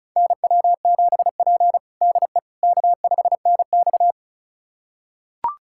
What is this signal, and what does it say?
Morse code, heard as a keyed tone, about 36 wpm, 700 Hz. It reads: NW7P DE K5NX
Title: USA Callsign DE Callsign - 4 Characters - 35wpm